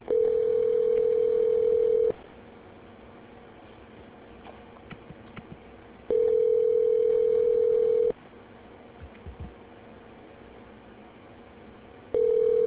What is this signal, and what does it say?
Ringback tone (North America), 440 + 480 Hz.